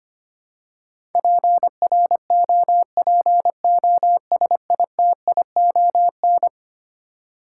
Morse code, heard as a keyed tone, about 25 wpm, 700 Hz. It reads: PROPOSITION
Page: morse-code-ninja-practice-25wpm